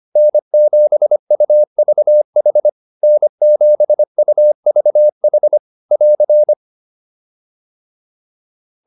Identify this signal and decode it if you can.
Morse code, heard as a keyed tone, about 25 wpm, 600 Hz. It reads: N7UVH N7UVH <AR>
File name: QSO105 25